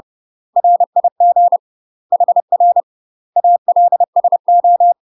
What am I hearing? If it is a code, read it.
Morse code, heard as a keyed tone, about 30 wpm, 700 Hz. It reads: RIG HR ALSO
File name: UK MASTER QSOs TRAINING / 30WPM / UK MASTER QSOs 0005